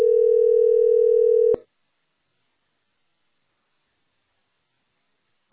Ringback tone (North America), 440 + 480 Hz.